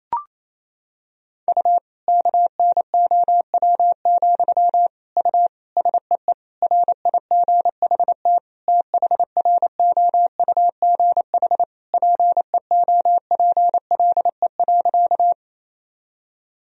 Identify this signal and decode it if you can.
Morse code, heard as a keyed tone, about 28 wpm, 700 Hz. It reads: U KNOW, U SEE RIGHT THROUGH PEOPLE.
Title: Sentences from Top 200 Words and Abbreviations 28wpm